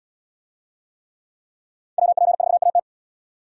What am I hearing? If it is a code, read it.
Morse code, heard as a keyed tone, about 68 wpm, 700 Hz. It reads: WP4AT